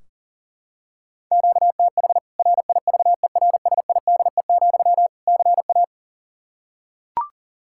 Morse code, heard as a keyed tone, about 40 wpm, 700 Hz. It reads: QTH RIVERSIDE, CA
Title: Practice QSO - Exchange 1 - Beginning - 40wpm